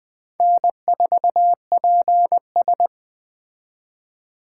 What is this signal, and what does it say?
Morse code, heard as a keyed tone, about 19 wpm, 700 Hz. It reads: N4PS